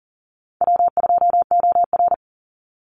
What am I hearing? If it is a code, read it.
Morse code, heard as a keyed tone, about 40 wpm, 700 Hz. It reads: W2OR
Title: Easy Call Signs Repeated 3x - 40wpm